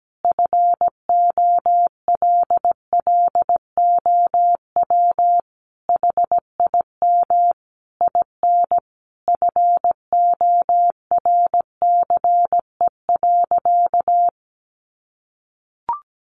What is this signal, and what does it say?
Morse code, heard as a keyed tone, about 17 wpm, 700 Hz. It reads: FOLLOW HIM IN FORCE.
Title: Sentences from Top 700 Words 17wpm